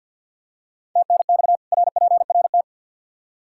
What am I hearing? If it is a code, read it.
Morse code, heard as a keyed tone, about 53 wpm, 700 Hz. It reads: TNX RPRT
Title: Practice QSO - Exchange 2 - Info - 50wpm